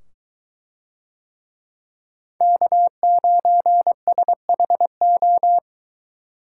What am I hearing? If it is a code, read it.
Morse code, heard as a keyed tone, about 23 wpm, 700 Hz. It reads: K9SHO